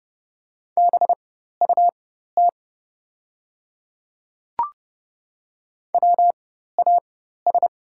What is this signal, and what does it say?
Morse code, heard as a keyed tone, about 30 wpm, spaced as for 14 wpm, 700 Hz. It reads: BUT WAS
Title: 4x Character Spacing - Top 100 Words - 30wpm